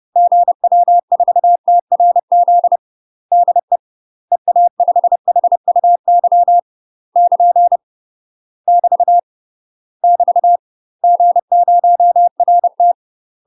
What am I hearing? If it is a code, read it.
Morse code, heard as a keyed tone, about 30 wpm, 700 Hz. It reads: GW4TRZ DE EA5HUY <KN> <BT> <BT> G0RT
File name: UK MASTER QSOs TRAINING / 30WPM / UK MASTER QSOs 0004